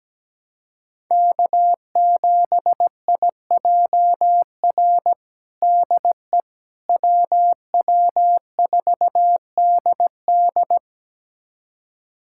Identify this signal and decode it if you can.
Morse code, heard as a keyed tone, about 17 wpm, 700 Hz. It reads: K7IJR DE WW4DD
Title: USA Callsign DE Callsign - 5 Characters - 17wpm